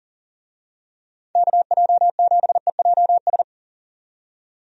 Morse code, heard as a keyed tone, about 40 wpm, 700 Hz. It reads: KJ7EJS
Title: USA Call Signs - 6 Characters - 40wpm